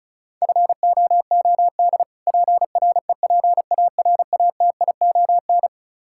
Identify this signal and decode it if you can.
Morse code, heard as a keyed tone, about 34 wpm, 700 Hz. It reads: FOOD PREPARATION
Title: K1USN SST QSO Practice 35wpm